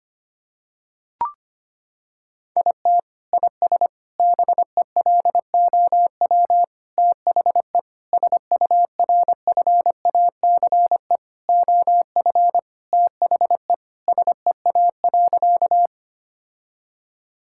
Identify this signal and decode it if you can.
Morse code, heard as a keyed tone, about 25 wpm, 700 Hz. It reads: IT IS BELOW THE SURFACE OF THE SEA.
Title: Sentences from Top 1000 Words - Review Part 5 - 25wpm